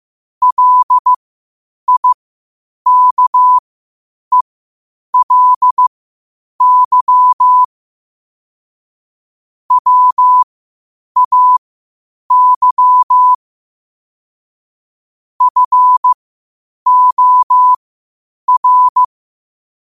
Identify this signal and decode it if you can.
Morse code, heard as a keyed tone, about 15 wpm, spaced as for 8 wpm, 1000 Hz. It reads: LIKELY WAY FOR
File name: qotdfiles / 8wpm